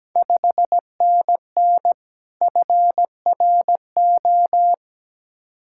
Morse code, heard as a keyed tone, about 17 wpm, 700 Hz. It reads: 5NN FRO
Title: Ontario QSO Party Exchange 17wpm